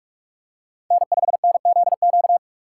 Morse code, heard as a keyed tone, about 41 wpm, 700 Hz. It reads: N5NZQ
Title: USA Call Signs - 5 Characters - 4x Character Spacing - 45wpm